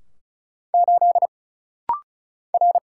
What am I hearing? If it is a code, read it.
Morse code, heard as a keyed tone, about 35 wpm, 700 Hz. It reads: 8 R
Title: Single Letter-Number - Rapid-Fire 35wpm